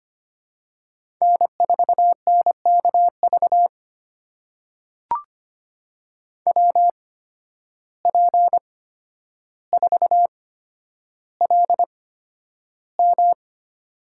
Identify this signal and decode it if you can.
Morse code, heard as a keyed tone, about 25 wpm, spaced as for 11 wpm, 700 Hz. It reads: N4NKV WP4LM
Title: USA Call Signs - 5 Characters - 8x Character Spacing - 25wpm